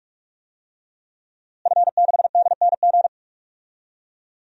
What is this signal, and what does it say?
Morse code, heard as a keyed tone, about 45 wpm, 700 Hz. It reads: F6DNG
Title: DXCC-100 Call Signs - 5 Characters - Repeated 3x - 45wpm